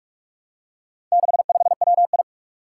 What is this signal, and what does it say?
Morse code, heard as a keyed tone, about 46 wpm, 700 Hz. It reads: 65WI
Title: Four Challenging Letter-Number Combinations 45wpm